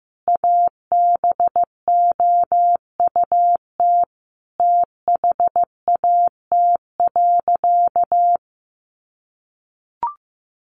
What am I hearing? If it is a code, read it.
Morse code, heard as a keyed tone, about 15 wpm, 700 Hz. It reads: ABOUT THAT.
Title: Sentences from Top 1200 Words 15wpm